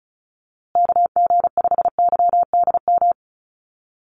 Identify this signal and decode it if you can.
Morse code, heard as a keyed tone, about 34 wpm, 700 Hz. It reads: KG5YDM
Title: USA Call Signs Repeated 3x 35wpm